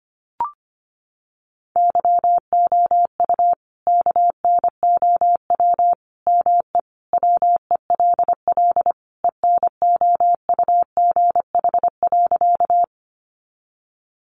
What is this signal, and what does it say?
Morse code, heard as a keyed tone, about 25 wpm, 700 Hz. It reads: YOU KNOW ME WELL ENOUGH.
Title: Sentences from Top 400 Words 25wpm